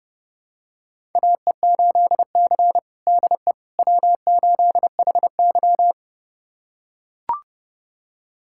Morse code, heard as a keyed tone, about 32 wpm, 700 Hz. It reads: AE8C DE W8HY
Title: USA Callsign DE Callsign - 4 Characters - 30wpm